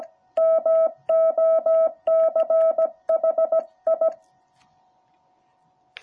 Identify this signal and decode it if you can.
Morse code, heard as a keyed tone, about 17 wpm, 650 Hz. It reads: MOCHI